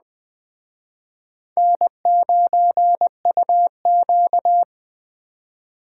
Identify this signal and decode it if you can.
Morse code, heard as a keyed tone, about 20 wpm, 700 Hz. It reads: N9UQ
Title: Speed Racing - USA 1x2 Call Signs 17wpm